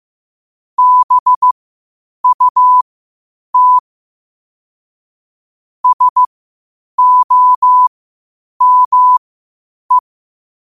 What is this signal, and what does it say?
Morse code, heard as a keyed tone, about 15 wpm, spaced as for 8 wpm, 1000 Hz. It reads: BUT SOME
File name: qotdfiles / 8wpm